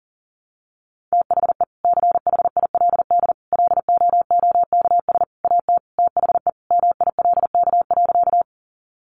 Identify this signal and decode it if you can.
Morse code, heard as a keyed tone, about 40 wpm, 700 Hz. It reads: THE CHILD LOOKS AT THE MILK.
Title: Additional Easy Phrases 40wpm